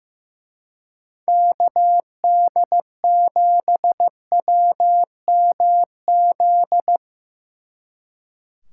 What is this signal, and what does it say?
Morse code, heard as a keyed tone, about 16 wpm, 700 Hz. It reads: KD7WMZ
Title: Speed Racing - USA 2x3 Call Signs 15wpm